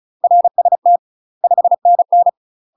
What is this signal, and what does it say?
Morse code, heard as a keyed tone, about 34 wpm, 700 Hz. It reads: RST 5NN
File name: UK MASTER QSOs TRAINING / 35WPM / UK MASTER QSOs 0040